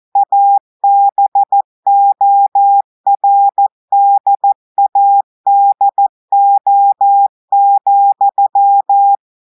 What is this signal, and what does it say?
Morse code, heard as a keyed tone, about 14 wpm, 800 Hz. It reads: ABORDADO,
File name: audios / portugais / portugais texte en clair 2 vitesse 14 mots minutes